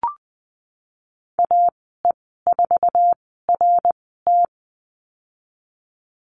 Morse code, heard as a keyed tone, about 19 wpm, spaced as for 14 wpm, 700 Hz. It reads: AE4RT
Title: 055 Lesson 10g - Letter r - Call Signs 20wpm